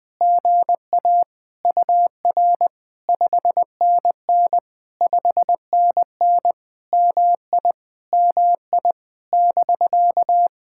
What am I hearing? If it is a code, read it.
Morse code, heard as a keyed tone, about 20 wpm, 700 Hz. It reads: GA UR 5NN 5NN MI MI <BK>
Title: POTA QSO Practice - 20wpm